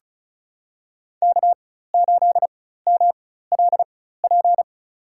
Morse code, heard as a keyed tone, about 36 wpm, spaced as for 16 wpm, 700 Hz. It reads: K8MLP
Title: USA Call Signs - 5 Characters - 4x Character Spacing - 35wpm